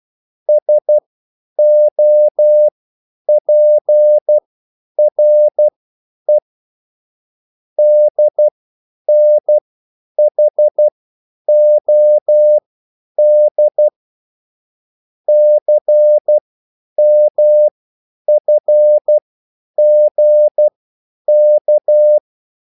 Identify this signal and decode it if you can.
Morse code, heard as a keyed tone, about 12 wpm, spaced as for 9 wpm, 600 Hz. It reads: SOPRE DNHOD CMFGK